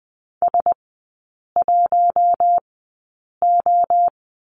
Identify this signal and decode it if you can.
Morse code, heard as a keyed tone, about 20 wpm, spaced as for 8 wpm, 700 Hz. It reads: S1O
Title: 060 Lesson 11e - Letter h - Three Characters 20wpm